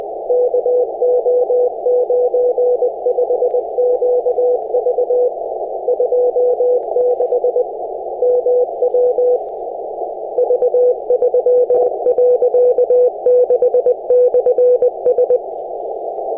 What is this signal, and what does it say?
Morse code, heard as a keyed tone, about 20 wpm, 500 Hz, with noise. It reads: KO95QV 26 MW V3.6/S
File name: attachments / 16.23.03 r2ajab